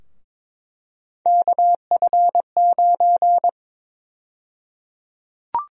Morse code, heard as a keyed tone, about 22 wpm, 700 Hz. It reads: KF9
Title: USA Call Sign Prefixes - 22wpm